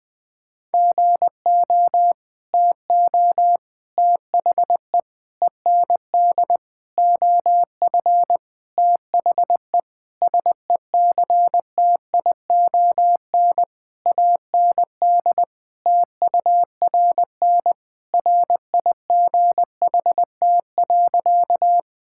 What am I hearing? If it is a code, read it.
Morse code, heard as a keyed tone, about 19 wpm, 700 Hz. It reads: GO TO THE END OF THE SECTION AND TURN RIGHT.
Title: Sentences from Top 1000 Words - Review Part 5 - 20wpm